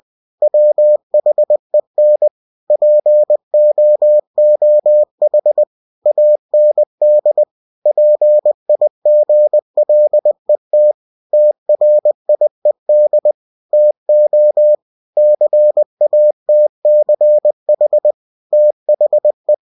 Morse code, heard as a keyed tone, about 20 wpm, 600 Hz. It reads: WHEN POOH AND PIGLET TRIED TO CATCH THE
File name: cw-books / winnie-the-pooh / winnie-20wpm-page-0011